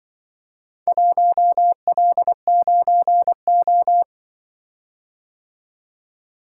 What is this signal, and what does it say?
Morse code, heard as a keyed tone, about 23 wpm, 700 Hz. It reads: 1L9O
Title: Speed-Racing - Four Letter-Number Combinations - 20wpm